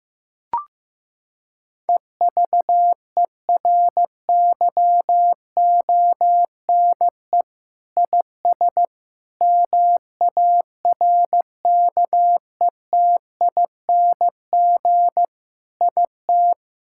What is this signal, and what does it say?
Morse code, heard as a keyed tone, about 15 wpm, 700 Hz. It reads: EVERYONE IS MARKETING IT
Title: Sentences from Top 1000 Words - Review Part 5 - 15wpm